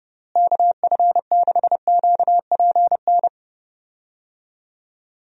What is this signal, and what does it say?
Morse code, heard as a keyed tone, about 32 wpm, 700 Hz. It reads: KF6QPN
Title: USA Call Signs - 6 Characters - Repeated 2x - 30wpm